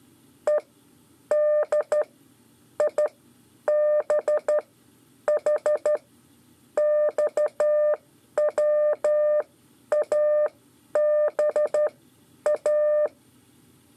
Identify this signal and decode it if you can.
Morse code, heard as a keyed tone, about 12 wpm, spaced as for 8 wpm, 600 Hz. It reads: EDIBHXWABA
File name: 5 LEZIONE RIPASSO 1